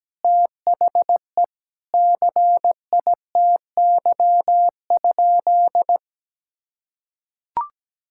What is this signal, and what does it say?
Morse code, heard as a keyed tone, about 17 wpm, 700 Hz. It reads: THE CITY?
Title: Sentences from Top 1600 Words 17wpm